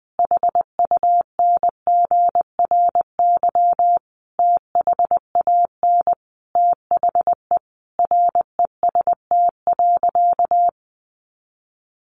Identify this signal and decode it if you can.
Morse code, heard as a keyed tone, about 20 wpm, 700 Hz. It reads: HUNGRY THAN THE REST.
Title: Sentences from Top 1400 Words 20wpm